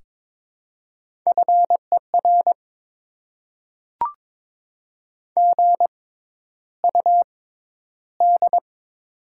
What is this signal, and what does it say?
Morse code, heard as a keyed tone, about 22 wpm, spaced as for 11 wpm, 700 Hz. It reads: FER GUD
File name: Top 20 QSO Elements - 6x Character Spacing - 22wpm